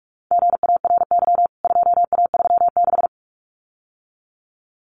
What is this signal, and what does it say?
Morse code, heard as a keyed tone, about 44 wpm, 700 Hz. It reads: GARY 2A36
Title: CWT Exchange 45wpm